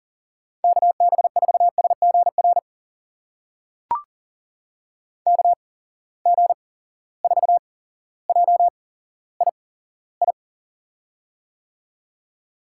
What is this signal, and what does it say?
Morse code, heard as a keyed tone, about 37 wpm, spaced as for 17 wpm, 700 Hz. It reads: KB4SGR KG4JII
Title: USA Call Signs - 6 Characters - 8x Character Spacing - 40wpm